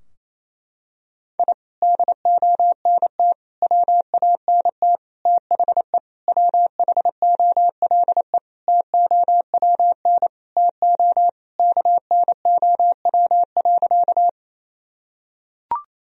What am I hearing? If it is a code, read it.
Morse code, heard as a keyed tone, about 27 wpm, 700 Hz. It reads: I DONT WANT THE WHOLE TOWN TO KNOW.
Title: Sentences from Top 1000 Words - Review Part 2 - 28wpm